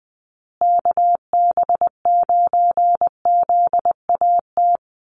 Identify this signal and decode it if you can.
Morse code, heard as a keyed tone, about 20 wpm, 700 Hz. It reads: KB9ZAT
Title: USA Call Signs Repeated 3x 20wpm